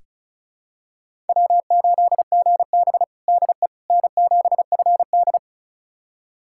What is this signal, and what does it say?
Morse code, heard as a keyed tone, about 34 wpm, 700 Hz. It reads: W8GB DE N7FD
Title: USA Callsign DE Callsign - 4 Characters - 35wpm